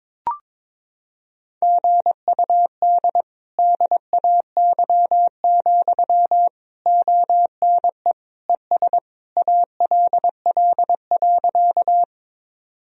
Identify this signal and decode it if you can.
Morse code, heard as a keyed tone, about 22 wpm, 700 Hz. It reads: GUD DAY, ONE ES ALL.
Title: Sentences from Top 200 Words and Abbreviations 22wpm